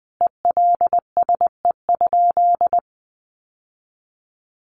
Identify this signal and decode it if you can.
Morse code, heard as a keyed tone, about 20 wpm, 700 Hz. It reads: ELSE?
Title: Sentences from Top 800 Words 20wpm